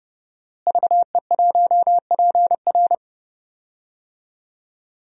Morse code, heard as a keyed tone, about 28 wpm, 700 Hz. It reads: VE1PR
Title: DXCC-100 Call Signs - 5 Characters - Repeated 3x - 30wpm